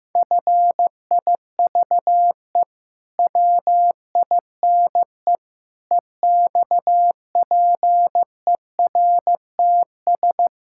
Morse code, heard as a keyed tone, about 15 wpm, 700 Hz. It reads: FIVE WINE EXPERTS
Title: Pangram Challenge v2 - 15wpm